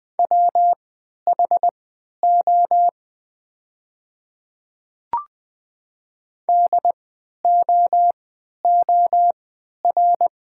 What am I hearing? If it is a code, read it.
Morse code, heard as a keyed tone, about 20 wpm, spaced as for 11 wpm, 690 Hz. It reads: WHO DOOR
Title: 3x Character Spacing - Top 300 Words - 20wpm